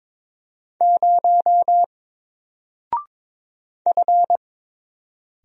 Morse code, heard as a keyed tone, about 22 wpm, 700 Hz. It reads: 0 F
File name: Single-Letter-Number-22wpm